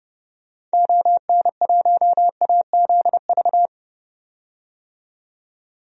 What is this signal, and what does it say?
Morse code, heard as a keyed tone, about 32 wpm, 700 Hz. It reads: ON1AZV
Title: DXCC-100 Call Signs - 6 Characters - 30wpm